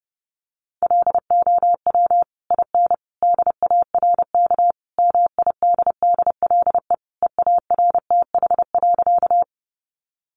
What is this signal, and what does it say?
Morse code, heard as a keyed tone, about 30 wpm, 700 Hz. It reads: LOW IN DARK MIDDLE EARTH.
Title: Sentences from Top 900 Words 30wpm